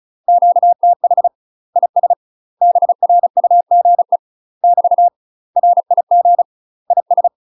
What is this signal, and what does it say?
Morse code, heard as a keyed tone, about 36 wpm, 700 Hz. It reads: QTH IS BRUGE <BT> RIG IS
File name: UK MASTER QSOs TRAINING / 35WPM / UK MASTER QSOs 0002